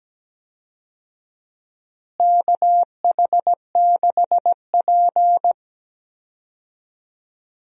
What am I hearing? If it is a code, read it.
Morse code, heard as a keyed tone, about 17 wpm, 700 Hz. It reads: KH6P
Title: DXCC-100 Call Signs - 4 Characters - Repeated 3x - 17wpm